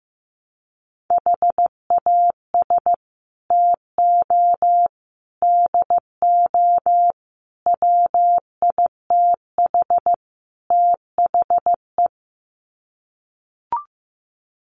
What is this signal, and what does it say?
Morse code, heard as a keyed tone, about 15 wpm, 700 Hz. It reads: HAS TO DO WITH THE
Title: Sets of 5 Words 15wpm